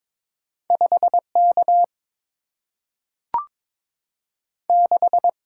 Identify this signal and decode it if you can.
Morse code, heard as a keyed tone, about 22 wpm, 700 Hz. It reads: 5K 6
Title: Single-Letter Single-Number 22wpm